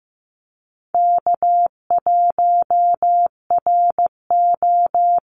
Morse code, heard as a keyed tone, about 15 wpm, 700 Hz. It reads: K1RO
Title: Easy Call Signs Repeated 2x - 15wpm